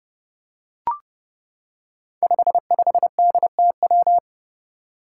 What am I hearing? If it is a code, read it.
Morse code, heard as a keyed tone, about 30 wpm, 700 Hz. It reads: 55DTW
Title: Five Challenging Letter-Number Combinations 30wpm